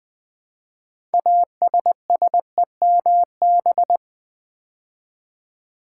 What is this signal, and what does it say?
Morse code, heard as a keyed tone, about 20 wpm, 700 Hz. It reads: ASSEMB
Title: Build A Word - 3 Letters At A Time - 8-Letter Words - 20wpm